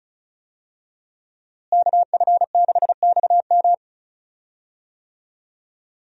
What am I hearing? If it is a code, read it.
Morse code, heard as a keyed tone, about 36 wpm, 700 Hz. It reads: KF6XM
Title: USA Call Signs - 5 Characters - Repeated 3x - 35wpm